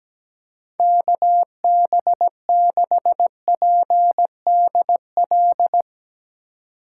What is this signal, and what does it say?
Morse code, heard as a keyed tone, about 20 wpm, 700 Hz. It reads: KB6PDL
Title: Speed Racing - USA Call Signs 17wpm